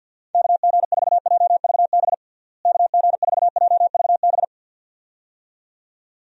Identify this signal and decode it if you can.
Morse code, heard as a keyed tone, about 51 wpm, 700 Hz. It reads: KG4JVB KG4JVB
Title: USA Call Signs - 6 Characters - Repeated 2x - 50wpm